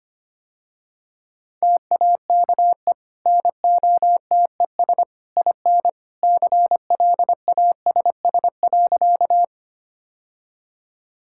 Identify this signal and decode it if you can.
Morse code, heard as a keyed tone, about 25 wpm, 700 Hz. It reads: TAKE NOTES IN CLASS.
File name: Easy Sentences with Hints - 25wpm